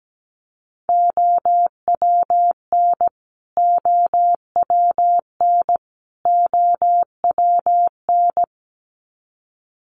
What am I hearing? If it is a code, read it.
Morse code, heard as a keyed tone, about 17 wpm, 700 Hz. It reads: OWN OWN OWN
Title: Top 300 Words Repeated 17wpm